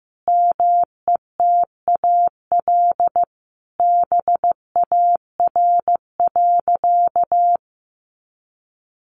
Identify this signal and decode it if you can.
Morse code, heard as a keyed tone, about 15 wpm, 700 Hz. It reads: METAL BAR.
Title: Additional Easy Phrases 15wpm